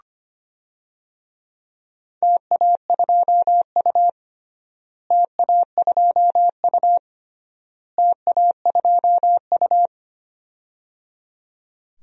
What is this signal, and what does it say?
Morse code, heard as a keyed tone, about 25 wpm, 700 Hz. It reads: TA2U TA2U TA2U
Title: DXCC-100 Call Signs - 4 Characters - Repeated 3x - 25wpm